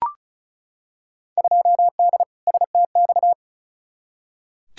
Morse code, heard as a keyed tone, about 34 wpm, 700 Hz. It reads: 2D STX
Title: Field Day Exchanges - 35wpm